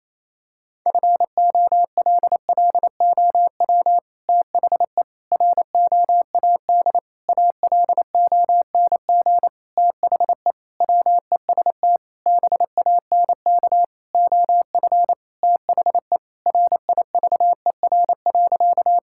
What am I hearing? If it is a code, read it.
Morse code, heard as a keyed tone, about 28 wpm, 700 Hz. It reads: FOLLOW THE ROAD ALONG THE WEST BANK OF THE RIVER.
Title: Sentences from Top 1000 Words - Review Part 6 - 28wpm